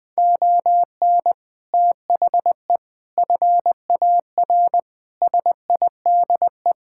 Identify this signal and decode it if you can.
Morse code, heard as a keyed tone, about 21 wpm, 700 Hz. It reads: ON THE FAR SIDE
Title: Build A Phrase - 5-Word Phrases - 20wpm